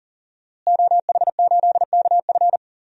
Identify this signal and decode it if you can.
Morse code, heard as a keyed tone, about 40 wpm, 700 Hz. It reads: OH8KF
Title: DXCC-100 Call Signs - 5 Characters - Repeated 3x - 40wpm